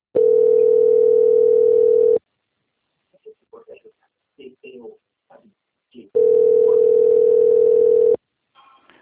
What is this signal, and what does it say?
Ringback tone (North America), 440 + 480 Hz, with speech.
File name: out- - - - - .79317